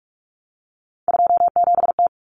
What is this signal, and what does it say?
Morse code, heard as a keyed tone, about 44 wpm, 700 Hz. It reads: 27T
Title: Common Cut Numbers - 3-Digit 45wpm